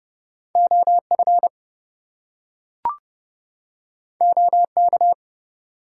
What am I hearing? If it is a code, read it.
Morse code, heard as a keyed tone, about 30 wpm, 700 Hz. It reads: OF OK
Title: Two Letter Words - 30wpm